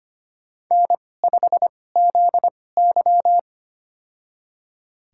Morse code, heard as a keyed tone, about 25 wpm, spaced as for 18 wpm, 700 Hz. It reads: N5ZY
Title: USA Call Signs - 4 Characters - 2x Character Spacing - 25wpm